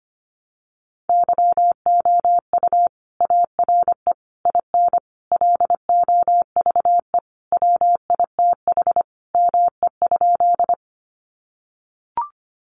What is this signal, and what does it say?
Morse code, heard as a keyed tone, about 25 wpm, 700 Hz. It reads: YOU ARE IN LOVE WITH ME?
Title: Sentences from Top 400 Words 25wpm